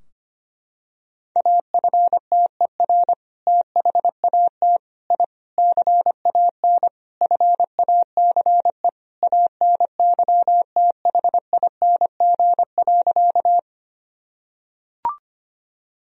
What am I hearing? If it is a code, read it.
Morse code, heard as a keyed tone, about 25 wpm, 700 Hz. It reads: AFTER THAT I CAN FACE ANYTHING.
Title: Sentences from Top 1000 Words - Review Part 2 - 25wpm